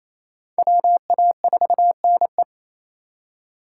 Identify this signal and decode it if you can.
Morse code, heard as a keyed tone, about 34 wpm, 700 Hz. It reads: WA4NE
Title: Speed Racing - USA Call Signs 28wpm